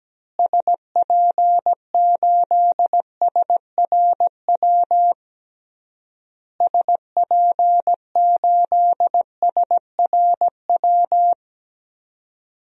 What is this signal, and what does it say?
Morse code, heard as a keyed tone, about 17 wpm, 700 Hz. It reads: SP8SRW SP8SRW
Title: DXCC-100 Call Signs - 6 Characters - Repeated 3x - 17wpm